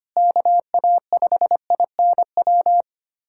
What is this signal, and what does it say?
Morse code, heard as a keyed tone, about 25 wpm, 700 Hz. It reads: KA5INW
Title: USA Callsign DE Callsign - 6 Characters - 25wpm